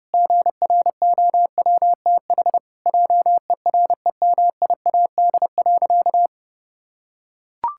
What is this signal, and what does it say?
Morse code, heard as a keyed tone, about 30 wpm, 700 Hz. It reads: GROWTH JEREMIAD.
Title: Pangram Challenge v2 - 30wpm